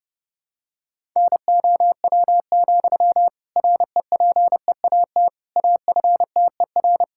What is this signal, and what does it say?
Morse code, heard as a keyed tone, about 30 wpm, 700 Hz. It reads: NOW, REPEAT AFTER
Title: Sentences from Top 1600 Words 30wpm